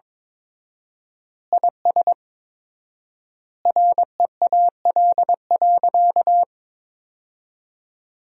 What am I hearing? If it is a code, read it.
Morse code, heard as a keyed tone, about 22 wpm, 700 Hz. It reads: IS REAL.
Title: 4x Word-Spacing - Sentences from Top 300 Words - 22wpm